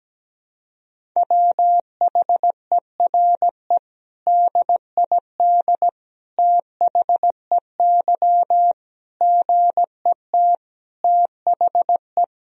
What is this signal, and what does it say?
Morse code, heard as a keyed tone, about 17 wpm, 700 Hz. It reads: WHERE DID THEY GET THE
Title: Sentences from Top 400 Words and Abbreviations 17wpm